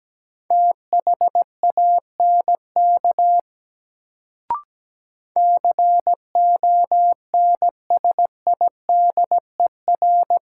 Morse code, heard as a keyed tone, about 17 wpm, 700 Hz. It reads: THANK CONSIDER
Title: Sentences from Top 600 Words - New Word Drill - 17wpm